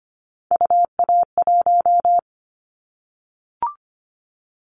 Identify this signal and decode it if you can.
Morse code, heard as a keyed tone, about 25 wpm, 700 Hz. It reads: UA1
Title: Callsign Prefixes for Top 100 DXCC - 25wpm